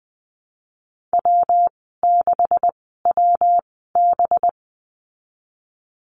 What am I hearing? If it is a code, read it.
Morse code, heard as a keyed tone, about 20 wpm, spaced as for 15 wpm, 700 Hz. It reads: W6WB
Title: 193 Lesson 31g - Letter v - Call Signs 20wpm